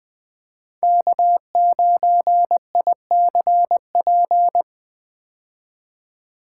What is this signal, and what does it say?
Morse code, heard as a keyed tone, about 20 wpm, 700 Hz. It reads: K9ICP
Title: 2023 POTA Hunter Call Signs - 20wpm